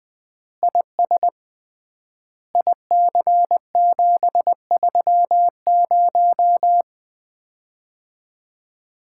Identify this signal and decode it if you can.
Morse code, heard as a keyed tone, about 20 wpm, 700 Hz. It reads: IS IC730
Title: Common QSO Phrases - 3x Word-Spacing - 20wpm